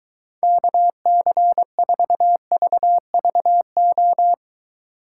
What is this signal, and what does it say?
Morse code, heard as a keyed tone, about 23 wpm, 700 Hz. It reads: KC4VVO